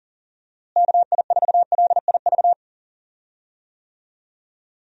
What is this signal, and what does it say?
Morse code, heard as a keyed tone, about 40 wpm, 700 Hz. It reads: KI4LIV
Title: USA Call Signs - 6 Characters - Repeated 3x - 40wpm